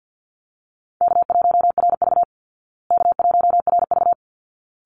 Morse code, heard as a keyed tone, about 50 wpm, 700 Hz. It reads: K1RV K1RV
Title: Call Signs Repeated 3x - 50wpm